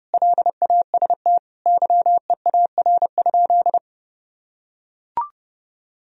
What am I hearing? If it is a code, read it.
Morse code, heard as a keyed tone, about 30 wpm, 700 Hz. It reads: LAST YEAR?
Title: Sentences from Top 1000 Words - Review Part 4 - 30wpm